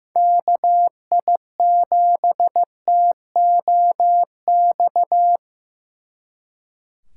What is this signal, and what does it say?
Morse code, heard as a keyed tone, about 18 wpm, 700 Hz. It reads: KI7TOX
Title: Speed Racing - USA 2x3 Call Signs 15wpm